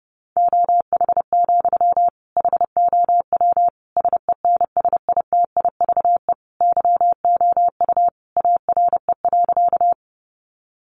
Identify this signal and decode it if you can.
Morse code, heard as a keyed tone, about 30 wpm, 700 Hz. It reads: OH, HOW SENSITIVE YOU ARE.
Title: Sentences from Top 1400 Words 30wpm